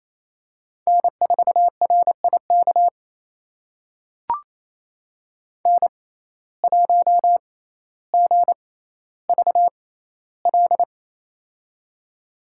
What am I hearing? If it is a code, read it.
Morse code, heard as a keyed tone, about 28 wpm, spaced as for 14 wpm, 700 Hz. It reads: N4RIK N1GVL
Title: USA Call Signs - 5 Characters - 6x Character Spacing - 28wpm